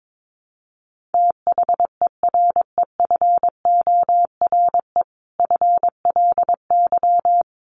Morse code, heard as a keyed tone, about 22 wpm, 700 Hz. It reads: THEREFORE FLY
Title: Sentences from Top 800 Words 22wpm